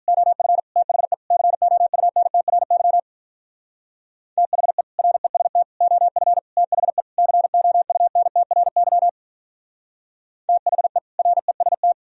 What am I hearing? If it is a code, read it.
Morse code, heard as a keyed tone, about 44 wpm, 700 Hz. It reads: OF THE COUNTRY THE REST OF THE COUNTRY THE REST
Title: Speed-Racing - Sets of 5 Words 35wpm